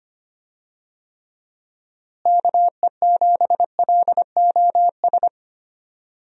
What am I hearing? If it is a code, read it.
Morse code, heard as a keyed tone, about 25 wpm, 700 Hz. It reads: KE7LOS
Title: USA Call Signs - 6 Characters - Repeated 3x - 25wpm